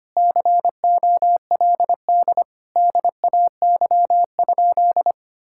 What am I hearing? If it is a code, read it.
Morse code, heard as a keyed tone, about 25 wpm, 700 Hz. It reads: COLD DAY?
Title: Sentences from Top 1000 Words - Review Part 6 - 25wpm